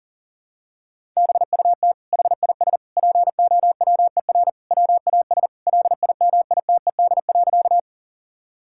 Morse code, heard as a keyed tone, about 40 wpm, 700 Hz. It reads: BUT HIS POWER WAS LIMITED.
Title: Sentences from Top 1000 Words - Review Part 6 - 40wpm